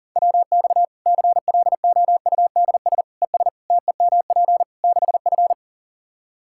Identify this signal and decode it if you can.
Morse code, heard as a keyed tone, about 37 wpm, 700 Hz. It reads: WX CLOUDS ES TEMP 6F
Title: Practice QSO - Exchange 2 - Info - 40wpm